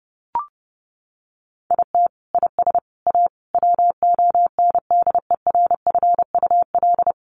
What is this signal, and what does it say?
Morse code, heard as a keyed tone, about 30 wpm, 700 Hz. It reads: IT IS A WONDERFUL
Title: Sentences from Top 1300 Words 30wpm